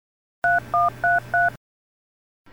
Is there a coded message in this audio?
Yes: DTMF from 0.4 to 1.5 s, heard as telephone touch tones, tones of about 180 ms. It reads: 3133